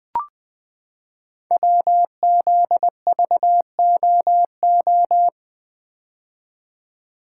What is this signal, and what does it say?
Morse code, heard as a keyed tone, about 20 wpm, 700 Hz. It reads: WZVOO
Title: Code Groups - 5 Letter:Numbers - 20wpm